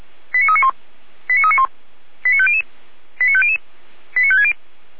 Five-tone selective call (ZVEI), heard as quick runs of five tones, tones of about 70 ms. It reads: 89391 89391 89500 89500 89560
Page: /455/SDR/ZVEI/89338/